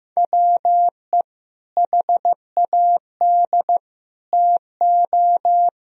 Morse code, heard as a keyed tone, about 15 wpm, 700 Hz. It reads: WE HAD TO
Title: Sentences from Top 1000 Words - Review Part 2 - 15wpm